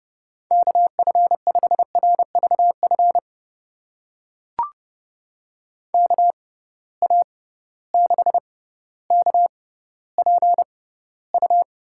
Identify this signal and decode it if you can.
Morse code, heard as a keyed tone, about 30 wpm, spaced as for 15 wpm, 700 Hz. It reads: KF5RVF KA6KPU
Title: USA Call Signs - 6 Characters - 6x Character Spacing - 30wpm